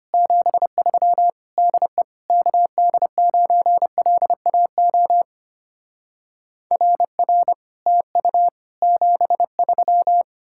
Morse code, heard as a keyed tone, about 33 wpm, 700 Hz. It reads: 73 DE KD9LAO RR TU 73